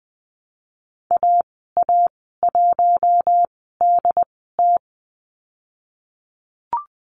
Morse code, heard as a keyed tone, about 20 wpm, spaced as for 14 wpm, 700 Hz. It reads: AA1DT
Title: 088 Lesson 15g - Number 5 - Call Signs 20wpm